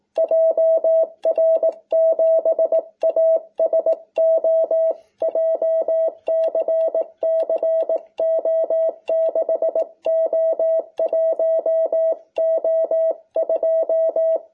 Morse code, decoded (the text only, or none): JR7ASO JCC O6O1O2